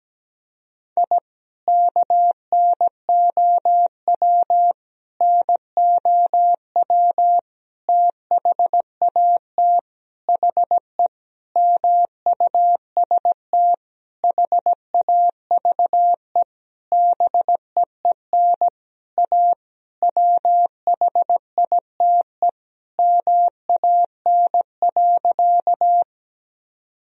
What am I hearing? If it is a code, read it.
Morse code, heard as a keyed tone, about 17 wpm, 700 Hz. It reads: I KNOW NOW THAT HE MUST HAVE BEEN A WHITE MAN.
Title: Sentences from Top 1000 Words - Review Part 1 - 17wpm